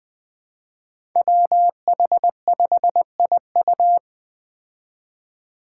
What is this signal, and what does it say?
Morse code, heard as a keyed tone, about 20 wpm, 700 Hz. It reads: WH5IU
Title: Five Challenging Letter-Number Combinations 20wpm